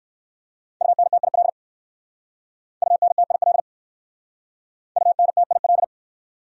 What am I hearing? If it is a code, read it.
Morse code, heard as a keyed tone, about 56 wpm, 700 Hz. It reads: UNTIL UNTIL UNTIL
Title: Speed-Racing Top 300 Words 45wpm